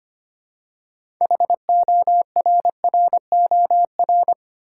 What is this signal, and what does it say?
Morse code, heard as a keyed tone, about 25 wpm, 700 Hz. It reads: HORROR
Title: Six Letter Words - 25wpm